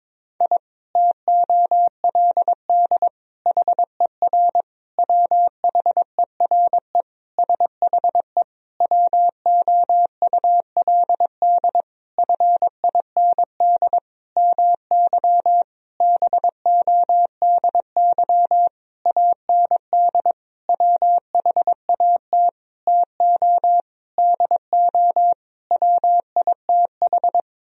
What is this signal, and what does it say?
Morse code, heard as a keyed tone, about 21 wpm, 700 Hz. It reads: I TOLD HER WHERE SHE WOULD FIND MY BODY AND WHAT TO DO WITH
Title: Sentences from Top 1000 Words - Review Part 2 - 22wpm